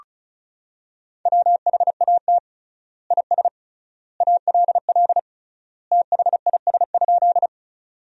Morse code, heard as a keyed tone, about 34 wpm, 700 Hz. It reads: WHAT IS ALL THIS?
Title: 3x Word-Spacing - Sentences from Top 100 Words - 35wpm